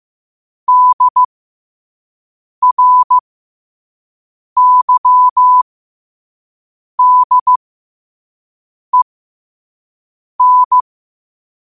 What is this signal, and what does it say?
Morse code, heard as a keyed tone, about 15 wpm, spaced as for 5 wpm, 1000 Hz. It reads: DRYDEN
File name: qotdfiles / 5wpm